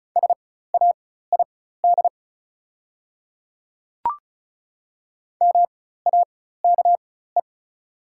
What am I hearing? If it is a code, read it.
Morse code, heard as a keyed tone, about 36 wpm, spaced as for 16 wpm, 700 Hz. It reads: SAID MAKE
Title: 4x Character Spacing - Top 100 Words - 35wpm